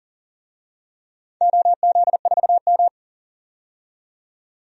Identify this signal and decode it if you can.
Morse code, heard as a keyed tone, about 40 wpm, 700 Hz. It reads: OZ4M
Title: DXCC-100 Call Signs - 4 Characters - 40wpm